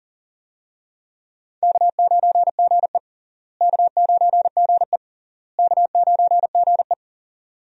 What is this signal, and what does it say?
Morse code, heard as a keyed tone, about 40 wpm, 700 Hz. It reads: K9GE K9GE K9GE
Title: USA Call Signs - 4 Characters - Repeated 3x - 40wpm